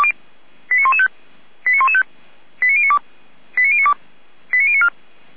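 Five-tone selective call (ZVEI), heard as quick runs of five tones, tones of about 70 ms. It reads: 89115 89115 89092 89092 89094
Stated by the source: ZVEI five-tone selective call received with an SDR.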